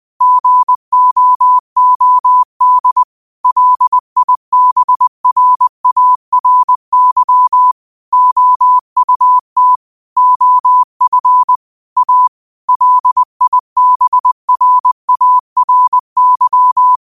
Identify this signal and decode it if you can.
Morse code, heard as a keyed tone, about 20 wpm, 1000 Hz. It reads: GOOD LIBRARY OUT OF A LIBRARY
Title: 20 WPM morse code quotes for Thu, 29 Feb 2024